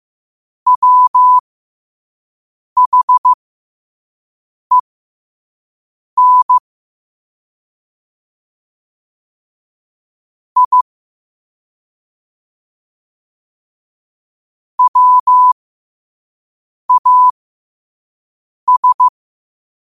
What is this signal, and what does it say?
Morse code, heard as a keyed tone, about 15 wpm, spaced as for 5 wpm, 1000 Hz. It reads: WHEN I WAS
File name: qotdfiles / 5wpm